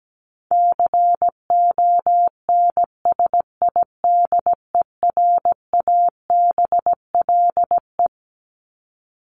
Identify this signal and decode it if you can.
Morse code, heard as a keyed tone, about 17 wpm, 700 Hz. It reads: CONSIDERABLE
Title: LY Suffix - Top 100 Words with Base Words 17wpm